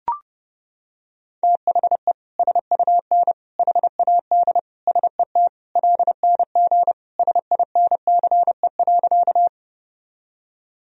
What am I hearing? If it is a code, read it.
Morse code, heard as a keyed tone, about 30 wpm, 700 Hz. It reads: THE SUN HAD SET LNG SINCE.
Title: Sentences from Top 300 Words and Abbreviations 30wpm